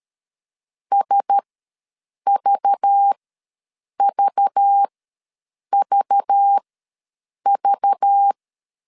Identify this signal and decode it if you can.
Morse code, heard as a keyed tone, about 13 wpm, spaced as for 7 wpm, 800 Hz. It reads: SVVVV
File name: LEZIONI / 5 LBVC / 5 LBV ws 12wpm 3015